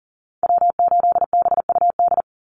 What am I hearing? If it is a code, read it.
Morse code, heard as a keyed tone, about 40 wpm, 700 Hz. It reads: W8BUD
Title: Call Signs Repeated 3x - 40wpm